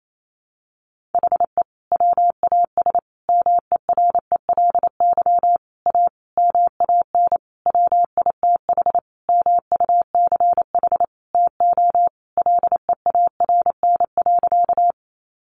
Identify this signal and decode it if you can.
Morse code, heard as a keyed tone, about 28 wpm, 700 Hz. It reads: HE WAS MERELY A MAN WITH MUCH TO LEARN.
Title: Sentences from Top 1400 Words 28wpm